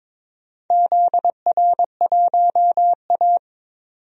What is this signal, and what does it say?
Morse code, heard as a keyed tone, about 22 wpm, 700 Hz. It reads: ZR1A
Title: DXCC-100 Call Signs - 4 Characters - 22wpm